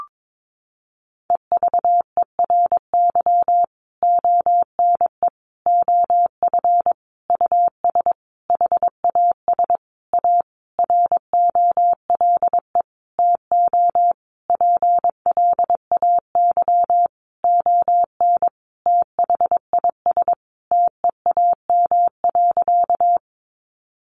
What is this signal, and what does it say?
Morse code, heard as a keyed tone, about 22 wpm, 700 Hz. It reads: EVERY ONE OF US HAS A ROLE TO PLAY ON THIS TEAM.
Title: Sentences from Top 1500 Words 22wpm